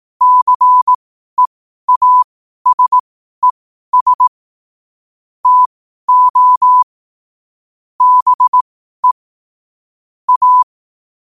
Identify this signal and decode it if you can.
Morse code, heard as a keyed tone, about 18 wpm, spaced as for 12 wpm, 1000 Hz. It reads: CEASES TO BE A
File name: qotdfiles / 12wpm